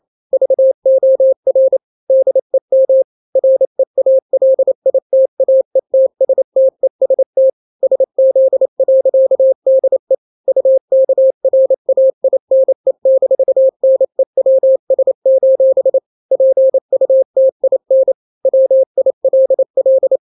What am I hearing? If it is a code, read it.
Morse code, heard as a keyed tone, about 28 wpm, 530 Hz. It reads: VOR DEM REALITAETSTEST SZ.DE UKRAINE-NEWS: PUTIN WILL
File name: daily / 27wpm